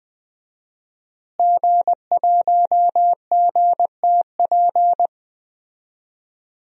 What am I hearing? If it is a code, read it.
Morse code, heard as a keyed tone, about 20 wpm, 700 Hz. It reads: G1GTP
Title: DXCC-100 Call Signs - 5 Characters - Repeated 3x - 20wpm